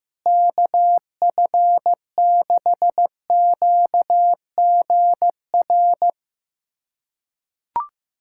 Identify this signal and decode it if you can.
Morse code, heard as a keyed tone, about 15 wpm, 700 Hz. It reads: KF6QGR
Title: USA Callsign DE Callsign 15wpm